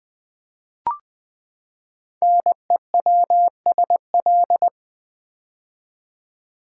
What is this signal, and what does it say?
Morse code, heard as a keyed tone, about 20 wpm, 700 Hz. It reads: NEWSL
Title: Build A Word - 5 Letters At A Time - 10-Letter Words - 20wpm